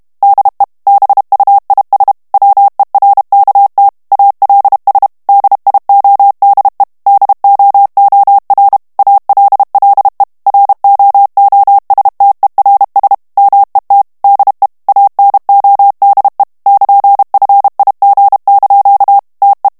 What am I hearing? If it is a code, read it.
Morse code, heard as a keyed tone, about 32 wpm, 800 Hz. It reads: DE BUIS WERKT ALS DIODE DOOR ALLE ROOSTERS MET DE ANODE <KN>FIG) TE
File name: HOLLANDAIS texte clair 2 vitesse 30 mots minute